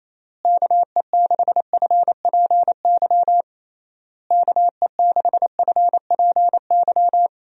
Morse code, heard as a keyed tone, about 28 wpm, 700 Hz. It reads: KE6FPY KE6FPY
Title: USA Call Signs - 6 Characters - Repeated 3x - 28wpm